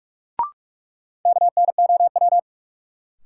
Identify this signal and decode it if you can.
Morse code, heard as a keyed tone, about 47 wpm, 700 Hz. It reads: KNOW
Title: Sentences from Top 100 Words - New Word Drill - 45wpm